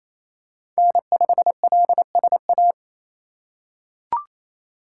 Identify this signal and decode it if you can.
Morse code, heard as a keyed tone, about 28 wpm, 700 Hz. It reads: N5LSA
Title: USA Call Signs - 2x Character Spacing - 28wpm